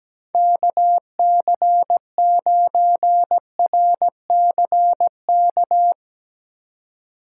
Morse code, heard as a keyed tone, about 17 wpm, 700 Hz. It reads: KC9RCK
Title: USA Call Signs - 6 Characters - 2x Character Spacing - 17wpm